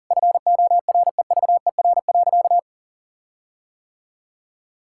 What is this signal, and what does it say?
Morse code, heard as a keyed tone, about 40 wpm, 700 Hz. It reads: FOREVER.
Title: Sentences from Top 1600 Words 40wpm